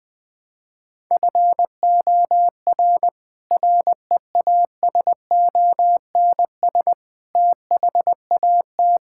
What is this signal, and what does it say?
Morse code, heard as a keyed tone, about 20 wpm, 700 Hz. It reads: FOR REASONS THAT
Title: Sets of 8 Words - 20wpm